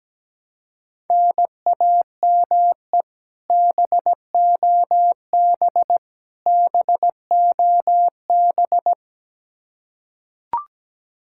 Morse code, heard as a keyed tone, about 17 wpm, 700 Hz. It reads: NAME BOB BOB
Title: Practice QSO - Exchange 1 - Beginning - 17wpm